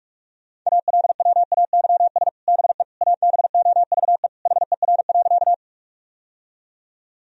Morse code, heard as a keyed tone, about 46 wpm, 700 Hz. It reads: ALWAYS BE ABOVE HER.
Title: Sentences from Top 1000 Words - Review Part 2 - 45wpm